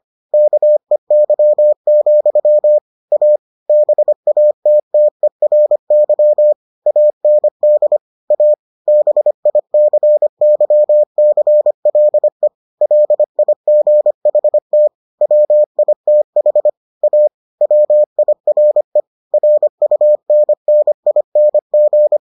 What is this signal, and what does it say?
Morse code, heard as a keyed tone, about 25 wpm, 600 Hz. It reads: KEY, A BATTERY AND A BICYCLE LIGHT WITH A WIRE RUNNING